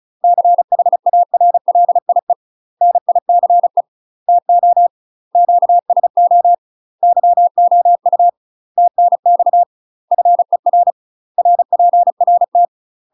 Morse code, heard as a keyed tone, about 34 wpm, 700 Hz. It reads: CHARLIE NICE TO QSO YOU TNX FER RPRT
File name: UK MASTER QSOs TRAINING / 35WPM / UK MASTER QSOs 0004